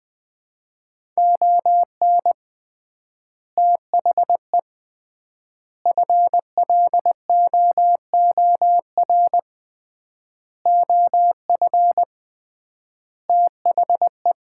Morse code, heard as a keyed tone, about 21 wpm, 700 Hz. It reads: ON THE FLOOR OF THE
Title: 3x Word-Spacing - Sets of 5 Words - 20wpm